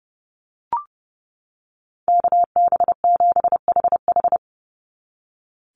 Morse code, heard as a keyed tone, about 30 wpm, 700 Hz. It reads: KB7HH
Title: NA SOTA Chaser Callsigns 30wpm